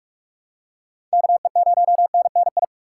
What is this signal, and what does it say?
Morse code, heard as a keyed tone, about 44 wpm, 700 Hz. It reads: KE0NNI
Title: USA Call Signs - 6 Characters - 6x Character Spacing - 45wpm